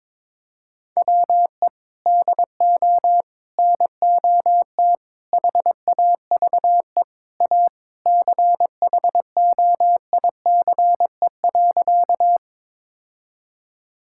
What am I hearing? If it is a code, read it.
Morse code, heard as a keyed tone, about 22 wpm, 700 Hz. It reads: WE DO NOT HAVE A CHOICE.
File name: practice / 22wpm / Sentences with 3-Word Phrases - 22wpm